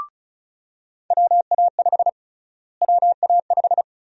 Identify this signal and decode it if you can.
Morse code, heard as a keyed tone, about 36 wpm, 700 Hz. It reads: WA5 WA5
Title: USA Call Sign Prefixes - Repeated 2x - 35wpm